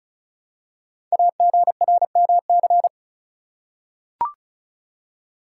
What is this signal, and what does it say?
Morse code, heard as a keyed tone, about 36 wpm, 690 Hz. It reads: AGRMC
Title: Code Groups - 5 Letters - 35wpm